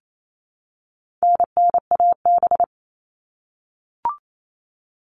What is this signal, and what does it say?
Morse code, heard as a keyed tone, about 28 wpm, 700 Hz. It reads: NNAB
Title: All Cut Numbers - 4-Digit 28wpm